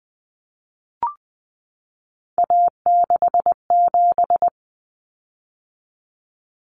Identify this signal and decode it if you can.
Morse code, heard as a keyed tone, about 20 wpm, 700 Hz. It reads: A67
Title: Common Cut Numbers - 3-Digit 20wpm